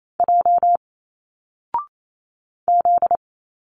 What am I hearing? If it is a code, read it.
Morse code, heard as a keyed tone, about 28 wpm, 700 Hz. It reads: J Z
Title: Single Letters 28wpm